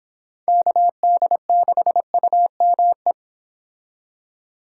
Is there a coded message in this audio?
Yes: Morse code from 0.5 to 3.1 s, heard as a keyed tone, about 26 wpm, 700 Hz. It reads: KD6UME